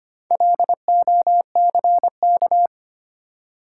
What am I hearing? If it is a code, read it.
Morse code, heard as a keyed tone, about 25 wpm, 690 Hz. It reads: LOCK
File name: practice / 25wpm / Easy Words with Hints - 25wpm